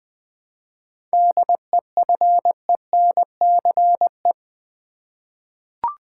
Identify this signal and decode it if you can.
Morse code, heard as a keyed tone, about 20 wpm, 700 Hz. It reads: DEFENCE
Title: 7 Letter Words - First 3 Letters Given - 20wpm